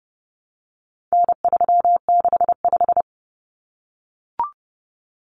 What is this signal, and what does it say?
Morse code, heard as a keyed tone, about 30 wpm, 700 Hz. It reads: N365
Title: Common Cut Numbers - 4-Digit 30wpm